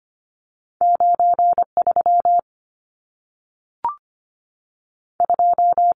Morse code, heard as a keyed tone, about 25 wpm, 700 Hz. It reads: 93 2
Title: Numbers - 2-Digit 25wpm